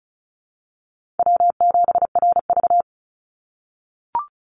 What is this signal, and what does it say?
Morse code, heard as a keyed tone, about 36 wpm, 700 Hz. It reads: W7RV
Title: NA SOTA Chaser Callsigns 35wpm